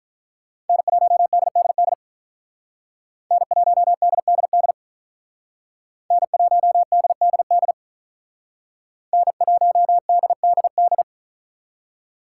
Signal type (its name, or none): Morse code